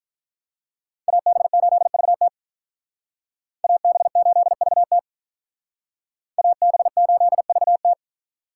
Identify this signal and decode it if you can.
Morse code, heard as a keyed tone, about 45 wpm, 700 Hz. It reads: AB8VT AB8VT AB8VT